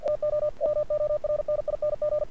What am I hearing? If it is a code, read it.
Morse code, heard as a keyed tone, about 48 wpm, 600 Hz. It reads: TO MORNING
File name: archived / CRSnet